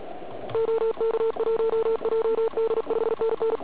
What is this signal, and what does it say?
Morse code, heard as a keyed tone, about 36 wpm, 440 Hz, with noise. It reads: OK1JD 5NN
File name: Archiv / CUC106 / cuc106 PILEUP